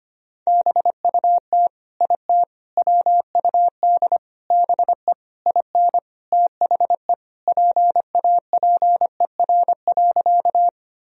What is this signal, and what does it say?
Morse code, heard as a keyed tone, about 25 wpm, 700 Hz. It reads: BUT IT WUD BE IN THE PAPER.
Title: Sentences from Top 300 Words and Abbreviations 25wpm